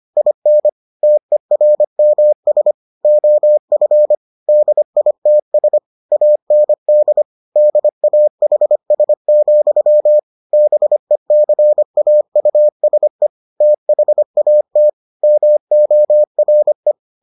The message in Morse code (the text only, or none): IN TERMS OF DITS AND DAHS, BECAUSE THAT MORE